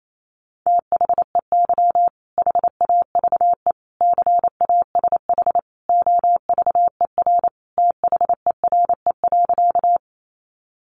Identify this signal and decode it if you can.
Morse code, heard as a keyed tone, about 28 wpm, 700 Hz. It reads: THEY HAVE CASH OVER THERE.
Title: Sentences from Top 700 Words 28wpm